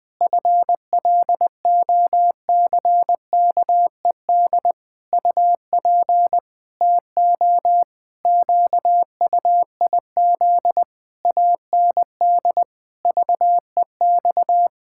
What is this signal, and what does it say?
Morse code, heard as a keyed tone, about 21 wpm, 700 Hz. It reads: FLOCKED UP TO QUIZ AND VEX
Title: Pangram Challenge v2 - 20wpm